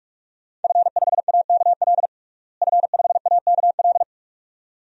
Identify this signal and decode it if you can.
Morse code, heard as a keyed tone, about 46 wpm, 700 Hz. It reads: F5AKL F5AKL
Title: DXCC-100 Call Signs - 5 Characters - Repeated 2x - 45wpm